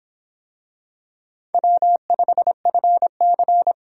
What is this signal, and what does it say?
Morse code, heard as a keyed tone, about 24 wpm, 700 Hz. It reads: W5FC